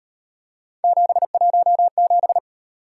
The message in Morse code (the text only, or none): 717